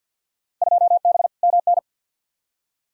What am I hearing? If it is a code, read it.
Morse code, heard as a keyed tone, about 48 wpm, 700 Hz. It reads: 2B MN
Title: Field Day Exchanges - 50wpm